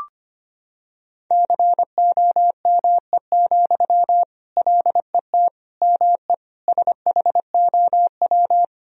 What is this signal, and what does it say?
Morse code, heard as a keyed tone, about 25 wpm, 700 Hz. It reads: COME, LET ME SHOW
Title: Sentences from Top 1000 Words - Review Part 2 - 25wpm